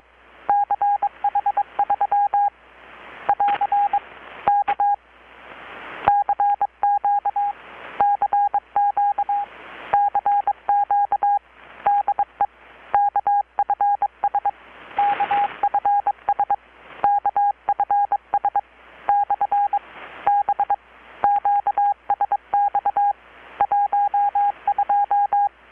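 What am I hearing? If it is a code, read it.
Morse code, heard as a keyed tone, about 22 wpm, 810 Hz. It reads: CH3 <AR> K CQ CQ CQ DE KFS KFS KFS/B QSX 12
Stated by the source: Morse from KPH's Night of Nights XX maritime radiotelegraphy event, received in Chiapas, Mexico about 3,800 km away on a Kenwood R-600 with a grounded 10 m long-wire antenna.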